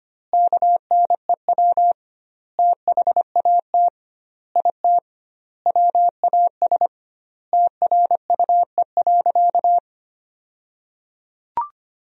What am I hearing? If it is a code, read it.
Morse code, heard as a keyed tone, about 25 wpm, 700 Hz. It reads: KNEW THAT IT WAS TRUE.
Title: 2x Word-Spacing - Sentences from Top 400 Words - 25wpm